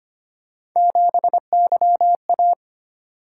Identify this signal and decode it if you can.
Morse code, heard as a keyed tone, about 25 wpm, 700 Hz. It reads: 7YA
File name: Three Letter-Number Combinations - 25wpm